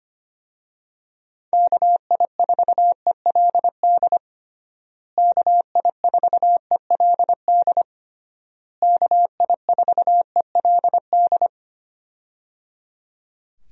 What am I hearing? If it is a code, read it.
Morse code, heard as a keyed tone, about 25 wpm, 700 Hz. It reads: KI4ELD KI4ELD KI4ELD
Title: USA Call Signs - 6 Characters - Repeated 3x - 25wpm